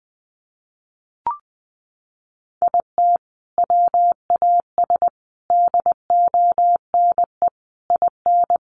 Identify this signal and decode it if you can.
Morse code, heard as a keyed tone, about 20 wpm, 700 Hz. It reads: IT WAS DONE IN
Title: Sentences from Top 1400 Words 20wpm